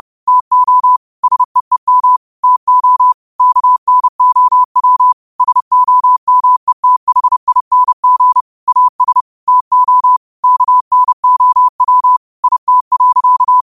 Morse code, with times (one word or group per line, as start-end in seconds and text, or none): 0.3-1.0 TO
1.2-2.2 SEEM
2.4-3.1 TO
3.4-5.1 KNOW
5.4-8.4 SOMETHING
8.7-9.2 AS
9.5-10.2 TO
10.4-12.2 KNOW
12.4-13.6 IT.